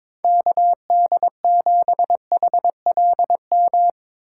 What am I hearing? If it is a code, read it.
Morse code, heard as a keyed tone, about 22 wpm, 700 Hz. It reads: KD7HLM
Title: USA Call Signs - 6 Characters - 3x Character Spacing - 22wpm